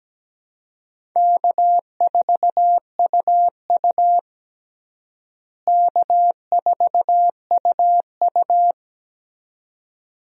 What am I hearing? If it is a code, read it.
Morse code, heard as a keyed tone, about 17 wpm, 700 Hz. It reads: K4UU K4UU
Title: USA Call Signs - 4 Characters - Repeated 2x - 17wpm